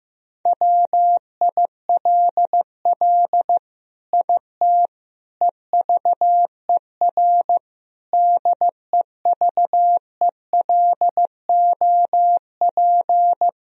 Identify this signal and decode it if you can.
Morse code, heard as a keyed tone, about 15 wpm, 700 Hz. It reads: WILL IT EVER DEVELOP
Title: Sentences from Top 1000 Words - Review Part 2 - 15wpm